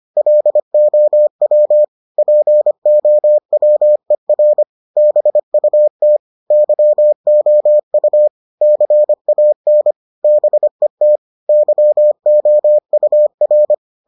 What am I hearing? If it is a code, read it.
Morse code, heard as a keyed tone, about 25 wpm, 600 Hz. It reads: LOW POWER BUT YOU CAN BET YOUR